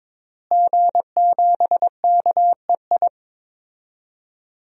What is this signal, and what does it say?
Morse code, heard as a keyed tone, about 21 wpm, 700 Hz. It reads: G7KEI
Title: DXCC-100 Call Signs - 5 Characters - Repeated 3x - 22wpm